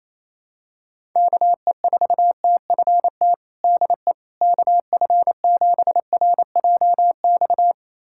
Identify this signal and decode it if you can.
Morse code, heard as a keyed tone, about 28 wpm, 700 Hz. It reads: KE4TFT DE KF7RJX
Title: USA Callsign DE Callsign - 6 Characters - 28wpm